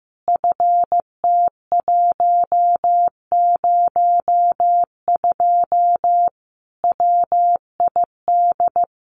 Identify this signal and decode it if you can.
Morse code, heard as a keyed tone, about 15 wpm, 700 Hz. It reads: FT102 WID
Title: Common QSO Phrases 15wpm